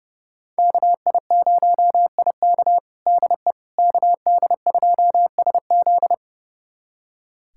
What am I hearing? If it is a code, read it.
Morse code, heard as a keyed tone, about 30 wpm, 700 Hz. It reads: KI0IK DE KD2SZ
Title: USA Callsign DE Callsign - 5 Characters - 30wpm